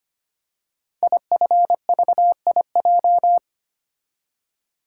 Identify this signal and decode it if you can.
Morse code, heard as a keyed tone, about 25 wpm, 700 Hz. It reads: IFVIJ
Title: Code Groups - 5 Letters - 25wpm